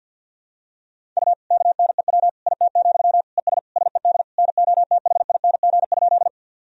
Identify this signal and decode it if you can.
Morse code, heard as a keyed tone, about 52 wpm, 700 Hz. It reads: U KNEW IT, ES SED NOTHING?
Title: Sentences from Top 500 Words and Abbreviations 50wpm